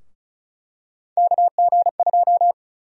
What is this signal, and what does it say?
Morse code, heard as a keyed tone, about 34 wpm, 700 Hz. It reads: KG2
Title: USA Call Sign Prefixes - 35wpm